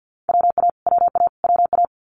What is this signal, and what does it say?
Morse code, heard as a keyed tone, about 49 wpm, 700 Hz. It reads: WA WA WA
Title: US States Repeated 3x 50wpm